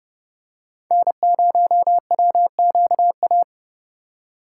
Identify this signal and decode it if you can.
Morse code, heard as a keyed tone, about 28 wpm, 700 Hz. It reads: N0WQA